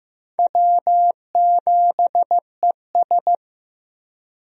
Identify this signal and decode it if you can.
Morse code, heard as a keyed tone, about 16 wpm, 700 Hz. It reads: W7ES
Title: USA Call Signs - 4 Characters - Repeated 2x - 15wpm